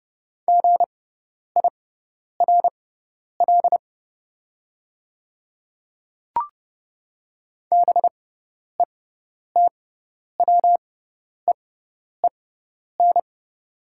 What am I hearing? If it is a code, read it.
Morse code, heard as a keyed tone, about 30 wpm, spaced as for 10 wpm, 700 Hz. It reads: GIRL BETWEEN
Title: 6x Character Spacing - Top 500 Words - 30wpm